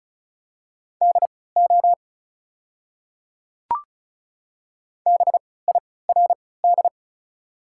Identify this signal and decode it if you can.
Morse code, heard as a keyed tone, about 34 wpm, spaced as for 20 wpm, 700 Hz. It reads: DO BIRD
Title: 3x Character Spacing - Top 500 Words - 35wpm